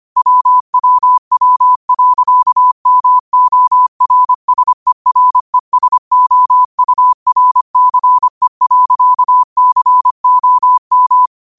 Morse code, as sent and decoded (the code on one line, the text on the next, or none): .-- .-- .-- .-.-.- -- --- .-. ... .-. ... --- ..- .-. -.-. .-.-.- -.-. --- --
WWW.MORSERESOURCE.COM